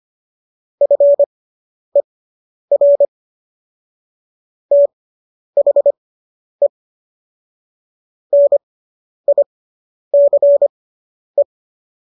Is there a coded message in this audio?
Yes: Morse code from 0.8 to 11.4 s, heard as a keyed tone, about 25 wpm, spaced as for 10 wpm, 570 Hz. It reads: FER THE NICE